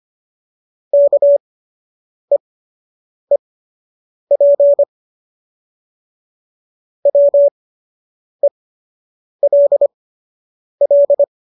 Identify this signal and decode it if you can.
Morse code, heard as a keyed tone, about 25 wpm, spaced as for 8 wpm, 570 Hz. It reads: KEEP WELL